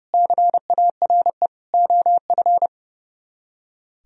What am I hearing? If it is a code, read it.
Morse code, heard as a keyed tone, about 37 wpm, 700 Hz. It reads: CARE OF
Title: Speed-Racing - Sets of 4 Words 30wpm